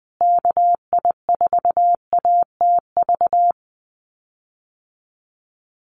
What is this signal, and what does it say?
Morse code, heard as a keyed tone, about 20 wpm, 700 Hz. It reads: KI4ATV
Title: USA Call Signs Repeated 3x 20wpm